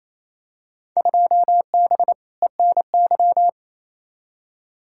Morse code, heard as a keyed tone, about 28 wpm, 700 Hz. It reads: 2B ENY
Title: Field Day Exchanges Encore - 28wpm